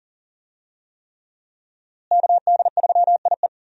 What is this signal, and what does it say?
Morse code, heard as a keyed tone, about 37 wpm, 700 Hz. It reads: KD3IE
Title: USA Call Signs - 5 Characters - Repeated 3x - 40wpm